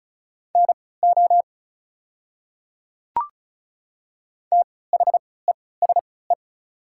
Morse code, heard as a keyed tone, about 34 wpm, spaced as for 20 wpm, 690 Hz. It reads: NO THESE
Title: 3x Character Spacing - Top 500 Words - 35wpm